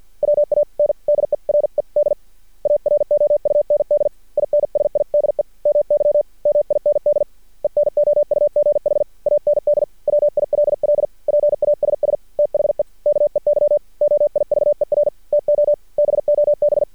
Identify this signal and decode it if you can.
Morse code, heard as a keyed tone, about 48 wpm, 590 Hz. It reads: WANDERED AROUND INSIDE MY MIND ENOUGH AND WILL PASS THE KEY OIVER TO BOB
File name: archived / CRSnet